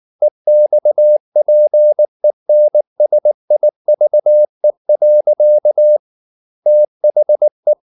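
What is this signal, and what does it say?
Morse code, heard as a keyed tone, about 19 wpm, 600 Hz. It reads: EXPENSIVE. THE
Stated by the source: Morse code at 19 wpm from CW med Gnister, reading Ken Follett's novel Never in English.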